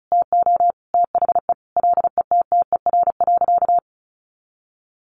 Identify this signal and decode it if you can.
Morse code, heard as a keyed tone, about 34 wpm, 700 Hz. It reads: TO THE LETTER.
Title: A Sound Of Thunder 35wpm